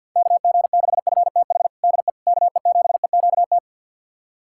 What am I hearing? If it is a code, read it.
Morse code, heard as a keyed tone, about 48 wpm, 700 Hz. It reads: KG6FTH DE KE7EQT
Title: USA Callsign DE Callsign - 6 Characters - 50wpm